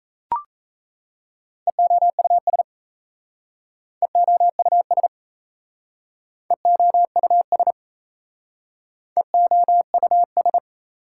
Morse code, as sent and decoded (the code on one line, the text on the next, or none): . --- ..- ... / . --- ..- ... / . --- ..- ... / . --- ..- ...
EOUS EOUS EOUS EOUS